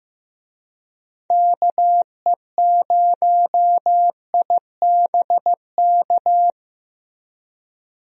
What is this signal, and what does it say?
Morse code, heard as a keyed tone, about 16 wpm, 700 Hz. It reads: KE0IBK
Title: USA Call Signs - 6 Characters - Repeated 2x - 15wpm